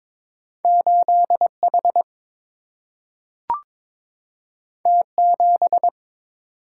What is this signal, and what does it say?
Morse code, heard as a keyed tone, about 22 wpm, 700 Hz. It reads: 8 H T 7
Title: Single-Letter Single-Number 22wpm